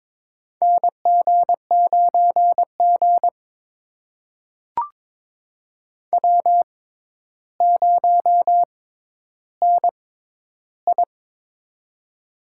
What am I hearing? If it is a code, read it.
Morse code, heard as a keyed tone, about 23 wpm, spaced as for 11 wpm, 700 Hz. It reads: NG9G W0NI
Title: USA Call Signs - 4 Characters - 6x Character Spacing - 22wpm